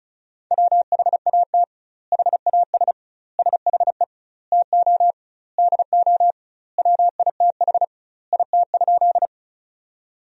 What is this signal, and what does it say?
Morse code, heard as a keyed tone, about 36 wpm, 700 Hz. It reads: WHAT HAS SHE TO DO WITH IT?
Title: 2x Word-Spacing - Sentences from Top 100 Words - 35wpm